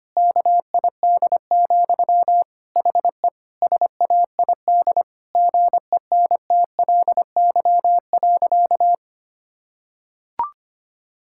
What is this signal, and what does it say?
Morse code, heard as a keyed tone, about 25 wpm, 700 Hz. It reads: KID, HE SAID GENTLY.
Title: Sentences from Top 1600 Words 25wpm